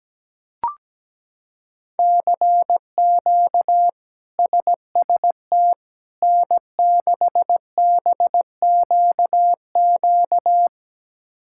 Morse code, heard as a keyed tone, about 17 wpm, 700 Hz. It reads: CQ SST N6BQQ
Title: K1USN SST QSO Practice - 17wpm